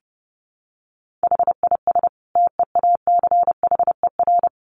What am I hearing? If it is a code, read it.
Morse code, heard as a keyed tone, about 28 wpm, 700 Hz. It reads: HIS TEACHER
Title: Additional Easy Phrases 30wpm